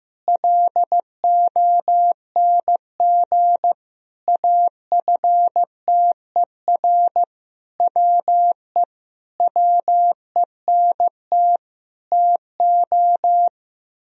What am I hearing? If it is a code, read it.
Morse code, heard as a keyed tone, about 15 wpm, 700 Hz. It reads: LONG AFTER WE WENT TO
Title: Sentences from Top 1000 Words - Review Part 2 - 15wpm